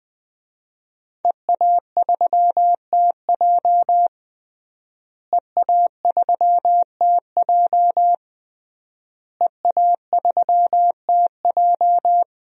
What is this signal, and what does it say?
Morse code, heard as a keyed tone, about 20 wpm, 700 Hz. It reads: EA3TJ EA3TJ EA3TJ
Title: DXCC-100 Call Signs - 5 Characters - Repeated 3x - 20wpm